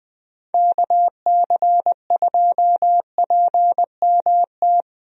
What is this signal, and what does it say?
Morse code, heard as a keyed tone, about 20 wpm, 700 Hz. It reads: KC2PMT
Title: USA Call Signs - 6 Characters - Repeated 2x - 20wpm